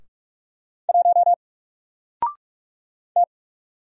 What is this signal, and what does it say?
Morse code, heard as a keyed tone, about 46 wpm, 700 Hz. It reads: 1 T
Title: Single Letter-Number 45wpm